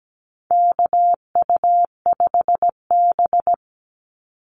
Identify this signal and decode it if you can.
Morse code, heard as a keyed tone, about 17 wpm, 700 Hz. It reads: KU5B
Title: Call Signs - 17wpm